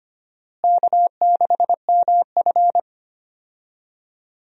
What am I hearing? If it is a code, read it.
Morse code, heard as a keyed tone, about 25 wpm, 700 Hz. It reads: K6MF
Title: USA Call Signs - 4 Characters - Repeated 3x - 25wpm